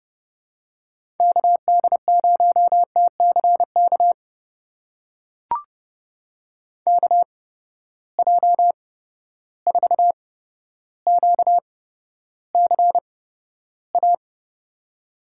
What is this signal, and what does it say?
Morse code, heard as a keyed tone, about 32 wpm, spaced as for 13 wpm, 700 Hz. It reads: KD0TCK KJ4QCA
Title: USA Call Signs - 6 Characters - 8x Character Spacing - 30wpm